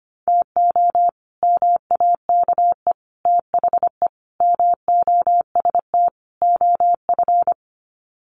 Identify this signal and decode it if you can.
Morse code, heard as a keyed tone, about 25 wpm, 700 Hz. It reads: TO MAKE THE MOST OF
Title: Sets of 5 Words Encore2 25wpm